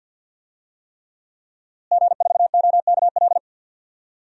Morse code, heard as a keyed tone, about 46 wpm, 700 Hz. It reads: G4OKL
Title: DXCC-100 Call Signs - 5 Characters - Repeated 3x - 50wpm